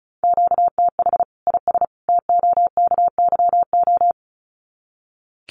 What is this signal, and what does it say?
Morse code, heard as a keyed tone, about 34 wpm, 700 Hz. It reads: QTH IS TOKYO
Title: Common QSO Phrases 35wpm